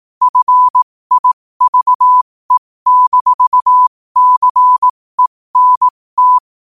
Morse code, heard as a keyed tone, about 18 wpm, spaced as for 15 wpm, 1000 Hz. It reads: FIVE-CENT
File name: qotdfiles / 15wpm